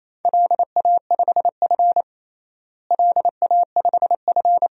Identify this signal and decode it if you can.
Morse code, heard as a keyed tone, about 28 wpm, 700 Hz. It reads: LA5F LA5F
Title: DXCC-100 Call Signs - 4 Characters - Repeated 2x - 28wpm